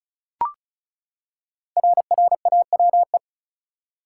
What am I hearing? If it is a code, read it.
Morse code, heard as a keyed tone, about 36 wpm, 700 Hz. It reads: RRAWE
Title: Code Groups - 5 Letters - 35wpm